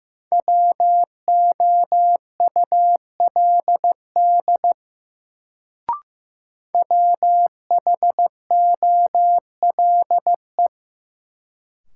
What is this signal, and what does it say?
Morse code, heard as a keyed tone, about 15 wpm, 700 Hz. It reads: WOULD WHOLE
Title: Sets of 2 Words - New Word Drill - 15wpm